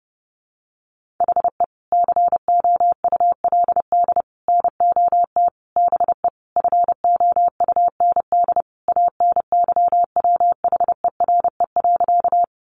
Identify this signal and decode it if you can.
Morse code, heard as a keyed tone, about 30 wpm, 700 Hz. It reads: HE COULD NOT BE FOUND ANYWHERE.
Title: Sentences from Top 1400 Words 30wpm